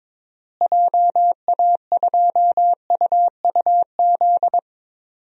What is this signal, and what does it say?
Morse code, heard as a keyed tone, about 22 wpm, 700 Hz. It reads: JA2UUZ
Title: DXCC-100 Call Signs - 6 Characters - Repeated 3x - 22wpm